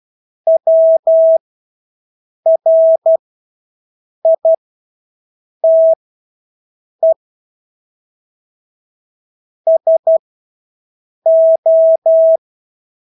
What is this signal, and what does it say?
Morse code, heard as a keyed tone, about 12 wpm, spaced as for 6 wpm, 650 Hz. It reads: WRITE SO
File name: Engelske ord 12-6 wpm